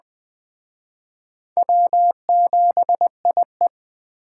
Morse code, heard as a keyed tone, about 21 wpm, 700 Hz. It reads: W7IE